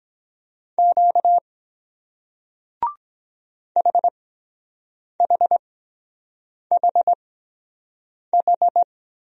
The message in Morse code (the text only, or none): Q HHHH